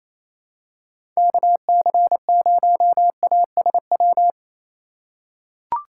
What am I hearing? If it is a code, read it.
Morse code, heard as a keyed tone, about 28 wpm, 700 Hz. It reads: KC0ASW
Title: USA Call Signs - 6 Characters - 3x Character Spacing - 28wpm